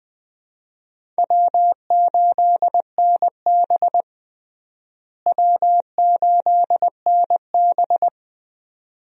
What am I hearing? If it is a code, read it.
Morse code, heard as a keyed tone, about 20 wpm, 700 Hz. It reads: W8NB W8NB
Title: USA Call Signs - 4 Characters - Repeated 2x - 20wpm